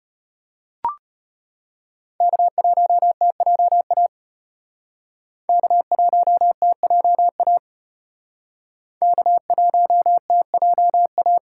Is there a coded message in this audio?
Yes: Morse code from 2.2 to 11.4 s, heard as a keyed tone, about 34 wpm, 700 Hz. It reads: K1TJA K1TJA K1TJA